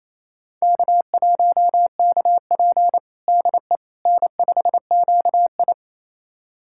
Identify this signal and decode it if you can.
Morse code, heard as a keyed tone, about 28 wpm, 700 Hz. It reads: K1KP DE N5QI
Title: USA Callsign DE Callsign - 4 Characters - 28wpm